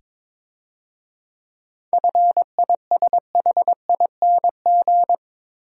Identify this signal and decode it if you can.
Morse code, heard as a keyed tone, about 26 wpm, 690 Hz. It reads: FISHING
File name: Easy Words - Speed-Racing with Hints - 22wpm